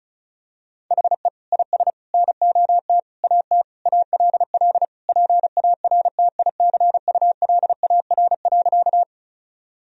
Morse code, heard as a keyed tone, about 34 wpm, 700 Hz. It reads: HE IS NOT AT ALL PARTICULAR.
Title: Sentences from Top 1000 Words - Review Part 4 - 35wpm